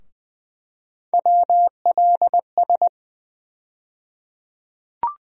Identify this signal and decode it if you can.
Morse code, heard as a keyed tone, about 20 wpm, 700 Hz. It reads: WLS
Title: USA Call Sign Postfixes - 3 Letters 20wpm